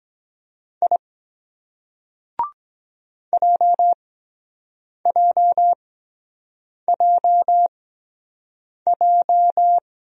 Morse code, heard as a keyed tone, about 23 wpm, spaced as for 7 wpm, 700 Hz. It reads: I JJJJ